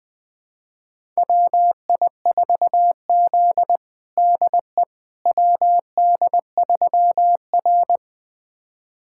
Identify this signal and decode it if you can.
Morse code, heard as a keyed tone, about 20 wpm, 700 Hz. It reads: WI4Z DE WD3R
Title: USA Callsign DE Callsign - 4 Characters - 20wpm